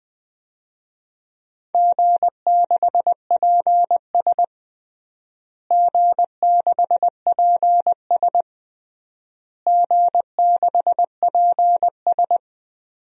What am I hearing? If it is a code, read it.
Morse code, heard as a keyed tone, about 21 wpm, 700 Hz. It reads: G6PS G6PS G6PS
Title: DXCC-100 Call Signs - 4 Characters - Repeated 3x - 20wpm